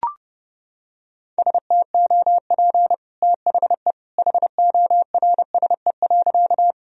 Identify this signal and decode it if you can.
Morse code, heard as a keyed tone, about 30 wpm, 700 Hz. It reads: STOP THE HORSE.
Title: Sentences from Top 1000 Words - Review Part 1 - 30wpm